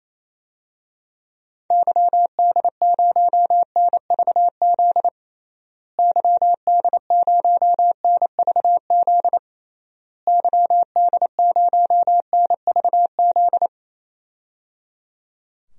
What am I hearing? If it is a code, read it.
Morse code, heard as a keyed tone, about 27 wpm, 700 Hz. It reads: YD0NVZ YD0NVZ YD0NVZ
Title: DXCC-100 Call Signs - 6 Characters - Repeated 3x - 28wpm